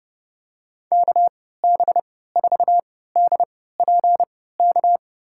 Morse code, heard as a keyed tone, about 32 wpm, spaced as for 17 wpm, 700 Hz. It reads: KB4DPK
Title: USA Call Signs - 6 Characters - 3x Character Spacing - 30wpm